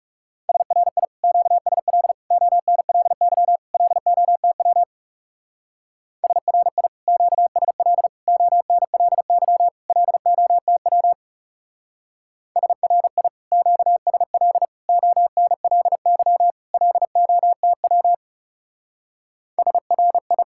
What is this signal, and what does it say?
Morse code, heard as a keyed tone, about 40 wpm, 700 Hz. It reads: SRI QSL ONLY LOTW SRI QSL ONLY LOTW SRI QSL ONLY LOTW SRI